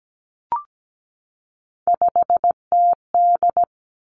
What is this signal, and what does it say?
Morse code, heard as a keyed tone, about 17 wpm, 700 Hz. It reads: 5TD
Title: Three Challenging Letter-Number Combinations 17wpm